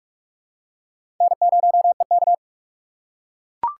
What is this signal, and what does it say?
Morse code, heard as a keyed tone, about 41 wpm, 700 Hz. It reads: N0EK
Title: USA Call Signs - 4x Character Spacing - 45wpm